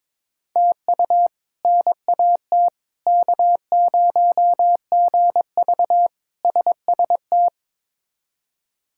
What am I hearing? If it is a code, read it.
Morse code, heard as a keyed tone, about 22 wpm, 700 Hz. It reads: TU NAT K0GV SST
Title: K1USN SST QSO Practice - 22wpm